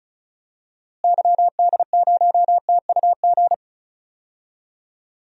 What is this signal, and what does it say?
Morse code, heard as a keyed tone, about 36 wpm, 700 Hz. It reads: YD0TUG
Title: DXCC-100 Call Signs - 6 Characters - Repeated 2x - 35wpm